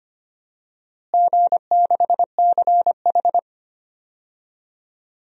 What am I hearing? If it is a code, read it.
Morse code, heard as a keyed tone, about 25 wpm, 700 Hz. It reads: G6CH
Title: DXCC-100 Call Signs - 4 Characters - Repeated 2x - 25wpm